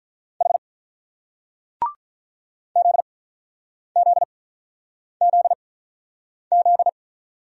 Morse code, heard as a keyed tone, about 44 wpm, spaced as for 9 wpm, 700 Hz. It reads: H ZZZZ